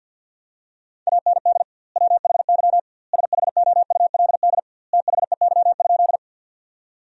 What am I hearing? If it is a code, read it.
Morse code, heard as a keyed tone, about 49 wpm, 700 Hz. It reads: AND WHY SHOULD THEY?
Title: 2x Word-Spacing - Sentences from Top 300 Words - 50wpm